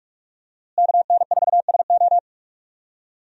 Morse code, heard as a keyed tone, about 46 wpm, 700 Hz. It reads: KN4SO
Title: USA Call Signs - 5 Characters - Repeated 3x - 45wpm